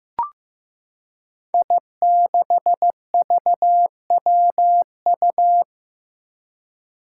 Morse code, heard as a keyed tone, about 15 wpm, 700 Hz. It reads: I6VWU
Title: Five Challenging Letter-Number Combinations 15wpm